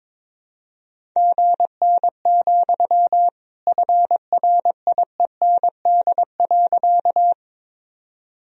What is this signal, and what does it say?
Morse code, heard as a keyed tone, about 22 wpm, 700 Hz. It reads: GN, FRIEND.
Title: Sentences from Top 300 Words and Abbreviations 22wpm